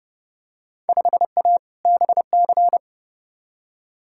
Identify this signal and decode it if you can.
Morse code, heard as a keyed tone, about 29 wpm, 700 Hz. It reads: 5A BC
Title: Field Day Exchanges - 30wpm